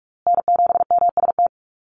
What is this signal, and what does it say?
Morse code, heard as a keyed tone, about 44 wpm, 690 Hz. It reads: N7MST
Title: USA Call Signs Repeated 2x 45wpm